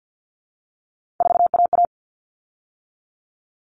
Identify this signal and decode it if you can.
Morse code, heard as a keyed tone, about 50 wpm, 700 Hz. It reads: 4AA
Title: All Cut Numbers - 3-Digit 50wpm